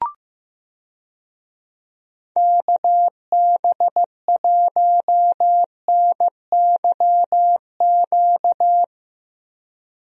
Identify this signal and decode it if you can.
Morse code, heard as a keyed tone, about 15 wpm, 700 Hz. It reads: KB1NYQ
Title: USA Call Signs - 6 Characters - Repeated 2x - 15wpm